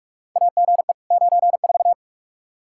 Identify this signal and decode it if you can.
Morse code, heard as a keyed tone, about 44 wpm, 700 Hz. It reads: AGE 94
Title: K1USN SST QSO Practice 45wpm